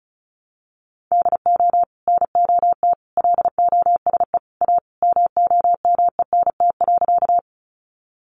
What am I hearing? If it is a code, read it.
Morse code, heard as a keyed tone, about 34 wpm, 700 Hz. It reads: DO NOT LOSE A MOMENT.
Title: Sentences from Top 1100 Words 35wpm